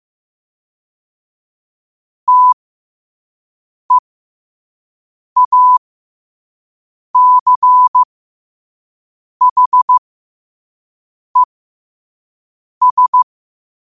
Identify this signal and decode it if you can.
Morse code, heard as a keyed tone, about 15 wpm, spaced as for 5 wpm, 1000 Hz. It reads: TEACHES
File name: 5wpm